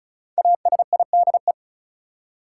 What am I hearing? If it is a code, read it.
Morse code, heard as a keyed tone, about 36 wpm, 700 Hz. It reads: ASIDE
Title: Five Letter Words - 35wpm